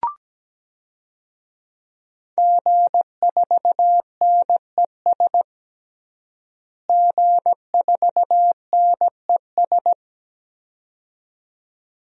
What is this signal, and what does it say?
Morse code, heard as a keyed tone, about 17 wpm, 700 Hz. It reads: G4NES G4NES
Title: DXCC-100 Call Signs - 5 Characters - Repeated 2x - 17wpm